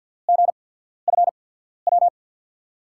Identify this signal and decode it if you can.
Morse code, heard as a keyed tone, about 48 wpm, spaced as for 13 wpm, 700 Hz. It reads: GFW
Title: USA Call Signs - 5 Characters - 8x Character Spacing - 50wpm